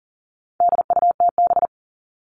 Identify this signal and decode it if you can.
Morse code, heard as a keyed tone, about 40 wpm, 700 Hz. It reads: DUTB
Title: All Cut Numbers - 4-Digit 40wpm